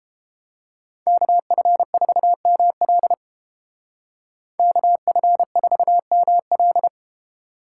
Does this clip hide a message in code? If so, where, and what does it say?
Yes: Morse code from 1.1 to 6.9 s, heard as a keyed tone, about 29 wpm, 700 Hz. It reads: KF4ML KF4ML